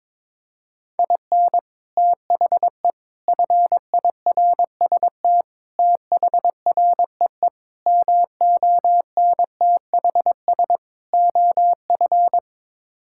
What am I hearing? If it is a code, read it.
Morse code, heard as a keyed tone, about 22 wpm, 700 Hz. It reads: IN THE FIRST THREE MONTHS OF
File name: practice / 22wpm / Sets of 6 Words - 22wpm